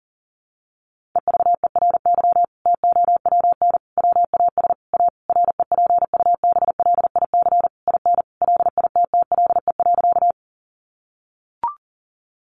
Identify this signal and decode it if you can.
Morse code, heard as a keyed tone, about 40 wpm, 700 Hz. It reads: EVERY TOWN WAS A REPUBLIC IN LITTLE.
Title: Sentences from Top 1500 Words 40wpm